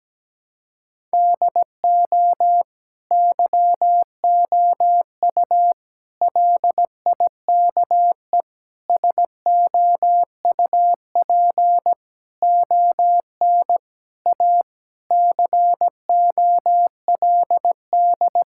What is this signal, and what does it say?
Morse code, heard as a keyed tone, about 17 wpm, 700 Hz. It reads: DO YOU LIKE SOUP ON A COLD
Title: Sentences from Top 1000 Words - Review Part 6 - 17wpm